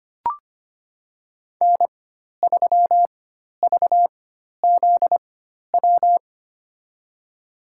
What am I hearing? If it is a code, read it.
Morse code, heard as a keyed tone, about 25 wpm, spaced as for 12 wpm, 700 Hz. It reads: N3VZW
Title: USA Call Signs - 5 Characters - 4x Character Spacing - 25wpm